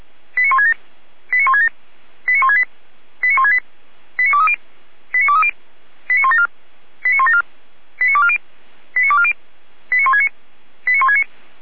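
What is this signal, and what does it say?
Five-tone selective call (ZVEI), heard as quick runs of five tones, tones of about 70 ms. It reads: 89167 89167 89168 89168 89230 89230 89174 89174 89240 89240 89169 89169
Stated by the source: ZVEI five-tone call 89121 received by SDR.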